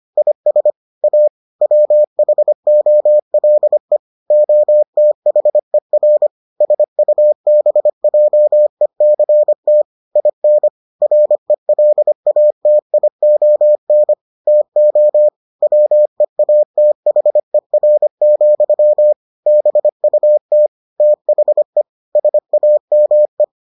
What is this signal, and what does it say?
Morse code, heard as a keyed tone, about 25 wpm, 600 Hz. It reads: IS A WHOLE OTHER SUBJECT IN RELATION TO WEATHER, BUT THE SAME